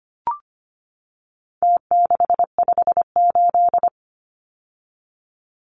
Morse code, heard as a keyed tone, about 25 wpm, 700 Hz. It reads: T658
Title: Numbers - Contest Style - 4-Digit 25wpm